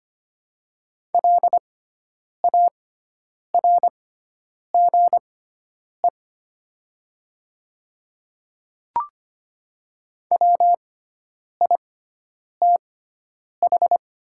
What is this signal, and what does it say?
Morse code, heard as a keyed tone, about 25 wpm, spaced as for 9 wpm, 700 Hz. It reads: LARGE WITH
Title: 6x Character Spacing - Top 300 Words - 25wpm